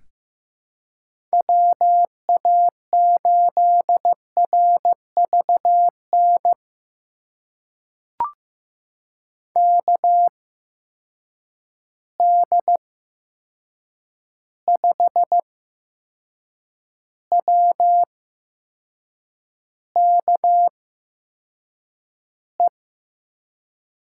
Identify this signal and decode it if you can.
Morse code, heard as a keyed tone, about 15 wpm, spaced as for 6 wpm, 700 Hz. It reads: WA8RVN KD5WKE
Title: USA Call Signs - 6 Characters - 8x Character Spacing - 15wpm